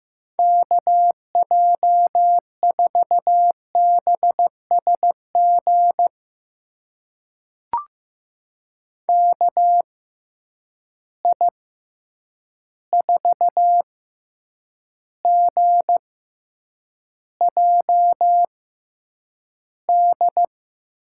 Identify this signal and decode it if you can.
Morse code, heard as a keyed tone, about 15 wpm, spaced as for 8 wpm, 700 Hz. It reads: KJ4BSG KI4GJD
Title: USA Call Signs - 6 Characters - 6x Character Spacing - 15wpm